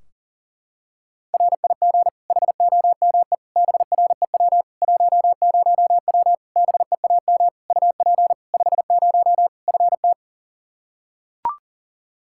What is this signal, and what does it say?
Morse code, heard as a keyed tone, about 37 wpm, 700 Hz. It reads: RIG HOME BREW 10W BEAM UP 50 FT
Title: Practice QSO - Exchange 2 - Info - 40wpm